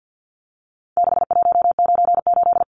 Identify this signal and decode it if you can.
Morse code, heard as a keyed tone, about 50 wpm, 700 Hz. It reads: 6198
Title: Numbers - 4-Digit 50wpm